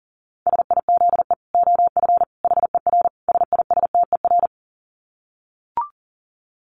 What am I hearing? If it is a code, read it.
Morse code, heard as a keyed tone, about 40 wpm, 700 Hz. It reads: SIZE OF HER SISTER
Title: Dit Challenge 40wpm